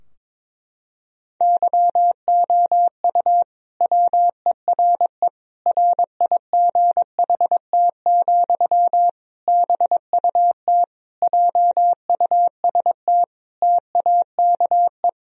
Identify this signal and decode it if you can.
Morse code, heard as a keyed tone, about 22 wpm, 700 Hz. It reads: YOU WERE RIGHT, BUT JUST TAKE
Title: Sentences from Top 1000 Words - Review Part 2 - 22wpm